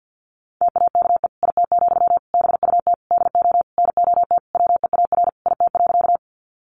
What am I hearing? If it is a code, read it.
Morse code, heard as a keyed tone, about 50 wpm, 700 Hz. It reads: TAKE IT, BUT DO NOT WEAR IT.
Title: Sentences from Top 1500 Words 50wpm